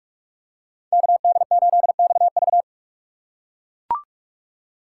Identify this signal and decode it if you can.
Morse code, heard as a keyed tone, about 44 wpm, 690 Hz. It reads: KD8XV
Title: USA Call Signs - 6x Character Spacing - 45wpm